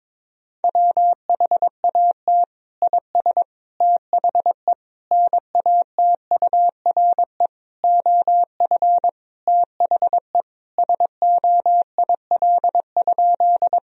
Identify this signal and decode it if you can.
Morse code, heard as a keyed tone, about 22 wpm, 700 Hz. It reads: WHAT IS THE NATURE OF THE SOIL?
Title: Sentences from Top 1000 Words - Review Part 4 - 22wpm